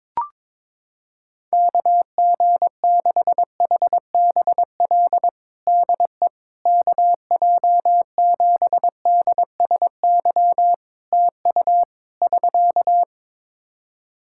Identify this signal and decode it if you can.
Morse code, heard as a keyed tone, about 22 wpm, 700 Hz. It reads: KG6HBL DE KJ7DSY TU <SK>
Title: K1USN SST QSO Practice 22wpm